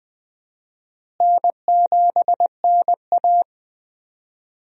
Morse code, heard as a keyed tone, about 20 wpm, 700 Hz. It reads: N7NA
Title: USA Call Signs - 4 Characters - Repeated 3x - 20wpm